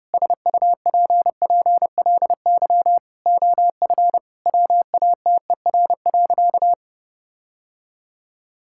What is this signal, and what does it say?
Morse code, heard as a keyed tone, about 30 wpm, 700 Hz. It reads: SUPPLY OF WATER.
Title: Sentences from Top 1000 Words - Review Part 5 - 30wpm